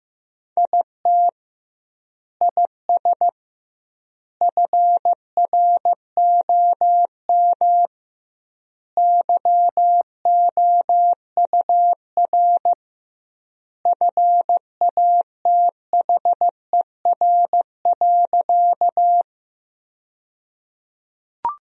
Morse code, heard as a keyed tone, about 15 wpm, 700 Hz. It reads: IT IS FROM YOUR FATHER.
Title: 2x Word-Spacing - Sentences from Top 300 Words - 15wpm